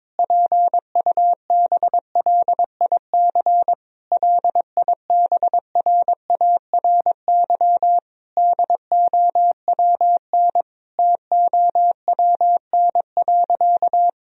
Morse code, heard as a keyed tone, about 21 wpm, 700 Hz. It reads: PUBLIC LIBRARY DOWN TOWN.
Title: Sentences from Top 1000 Words - Review Part 4 - 22wpm